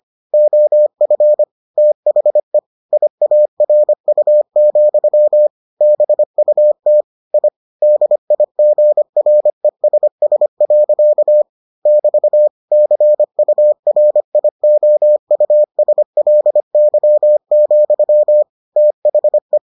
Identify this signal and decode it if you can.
Morse code, heard as a keyed tone, about 25 wpm, 600 Hz. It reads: OF THE IARU, BUT I DIGRESS. <BT> CURIOUSLY, THE